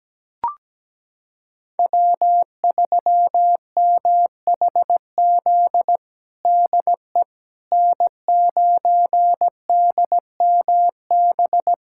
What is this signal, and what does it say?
Morse code, heard as a keyed tone, about 17 wpm, 700 Hz. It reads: W3MHZ DE N9DMB
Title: USA Callsign DE Callsign - 5 Characters - 17wpm